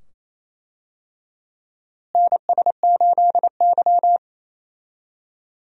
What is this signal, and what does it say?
Morse code, heard as a keyed tone, about 28 wpm, 700 Hz. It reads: NS8Y
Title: USA Call Signs - 4 Characters - Repeated 2x - 28wpm